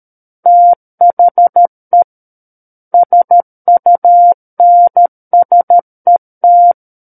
Morse code, heard as a keyed tone, about 13 wpm, 700 Hz. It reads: THE SUNSET
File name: Book One / 04-Chapter 4 - 13 WPM